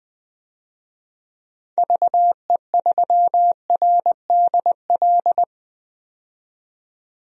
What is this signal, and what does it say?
Morse code, heard as a keyed tone, about 20 wpm, 700 Hz. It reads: VE3RDL
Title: DXCC-100 Call Signs - 6 Characters - 20wpm